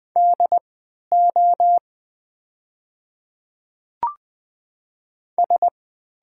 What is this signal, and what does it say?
Morse code, heard as a keyed tone, about 20 wpm, spaced as for 11 wpm, 690 Hz. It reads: DO S
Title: 3x Character Spacing - Top 100 Words - 20wpm